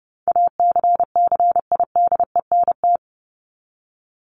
Morse code, heard as a keyed tone, about 30 wpm, 700 Hz. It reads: ACCIDENT
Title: Top 1751-2000 Words 30wpm